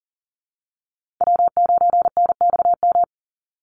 Morse code, heard as a keyed tone, about 40 wpm, 700 Hz. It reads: W9NXM
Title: Call Signs Repeated 2x - 40wpm